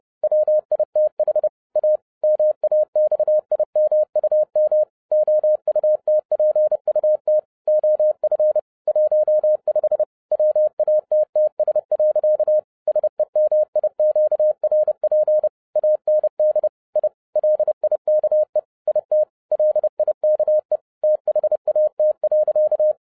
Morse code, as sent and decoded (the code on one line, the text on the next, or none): .-- .. - .... / .- / -- .- -..- .. -- ..- -- / --- ..- - .--. ..- - / --- ..-. / .---- ..... / .-- .- - - ... .-.-.- / ... -- .. --.- .-. .--. / .- -. -.. / .. / .-.. .. -.- . / .. - / .-.. .. -.- . / - .... .- - .-.-.-
WITH A MAXIMUM OUTPUT OF 15 WATTS. SEMIQRP AND I LIKE IT LIKE THAT.